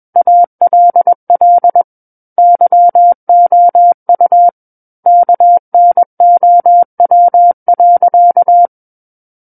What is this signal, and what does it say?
Morse code, heard as a keyed tone, about 21 wpm, 700 Hz. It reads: ALL YOU KNOW.
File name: Book One / 12-Chapter 12 - 21 WPM